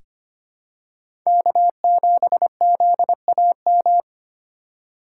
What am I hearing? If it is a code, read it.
Morse code, heard as a keyed tone, about 25 wpm, 700 Hz. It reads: K7ZAM
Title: USA Call Signs - 5 Characters - 3x Character Spacing - 25wpm